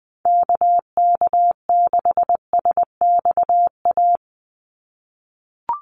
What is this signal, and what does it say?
Morse code, heard as a keyed tone, about 20 wpm, 700 Hz. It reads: KK6SXA
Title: USA Call Signs 20wpm